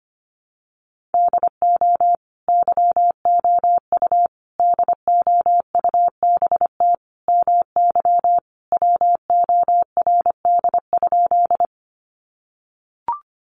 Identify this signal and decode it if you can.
Morse code, heard as a keyed tone, about 25 wpm, 700 Hz. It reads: DO YOU DOUBT MY WORD?
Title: Sentences from Top 1500 Words 25wpm